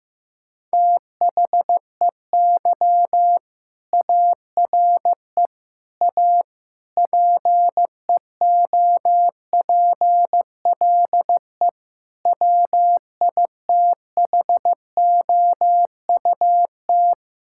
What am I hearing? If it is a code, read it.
Morse code, heard as a keyed tone, about 15 wpm, 700 Hz. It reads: THEY ARE A PEOPLE WITHOUT
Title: Sentences from Top 1000 Words - Review Part 6 - 15wpm